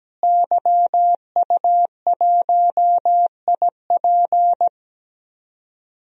Morse code, heard as a keyed tone, about 17 wpm, 700 Hz. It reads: YU1IP
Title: DXCC-100 Call Signs - 5 Characters - Repeated 3x - 17wpm